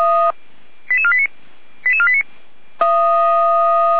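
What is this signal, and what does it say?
Five-tone selective call (ZVEI), heard as quick runs of five tones, tones of about 70 ms. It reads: 88489 88489
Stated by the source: received on SDR, ZVEI five-tone selective call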